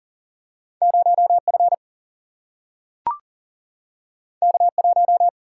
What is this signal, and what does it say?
Morse code, heard as a keyed tone, about 42 wpm, 700 Hz. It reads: 0F K1
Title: Single-Letter Single-Number 40wpm